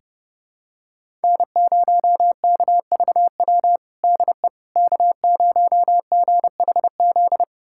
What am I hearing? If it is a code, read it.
Morse code, heard as a keyed tone, about 32 wpm, 700 Hz. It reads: N0KVW DE K0GHZ
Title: USA Callsign DE Callsign - 5 Characters - 30wpm